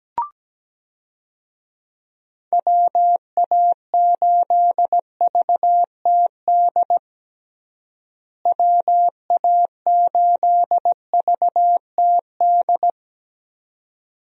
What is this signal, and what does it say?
Morse code, heard as a keyed tone, about 17 wpm, 700 Hz. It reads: WA8VTD WA8VTD
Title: USA Call Signs - 6 Characters - Repeated 2x - 17wpm